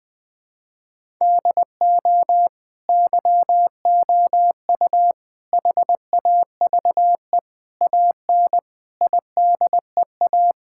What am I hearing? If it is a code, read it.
Morse code, heard as a keyed tone, about 20 wpm, 700 Hz. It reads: DO YOU HAVE AN IDEA
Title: Sets of 9 Words - 20wpm